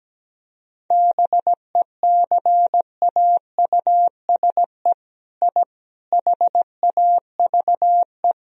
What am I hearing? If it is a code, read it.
Morse code, heard as a keyed tone, about 17 wpm, 700 Hz. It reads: BECAUSE I HAVE
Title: Sentences from Top 1000 Words - Review Part 4 - 17wpm